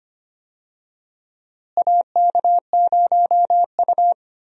Morse code, heard as a keyed tone, about 25 wpm, 700 Hz. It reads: AK0U
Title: USA Call Signs - 4 Characters - Repeated 3x - 25wpm